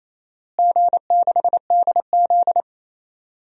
Morse code, heard as a keyed tone, about 27 wpm, 700 Hz. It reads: G6DZ
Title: DXCC-100 Call Signs - 4 Characters - Repeated 3x - 28wpm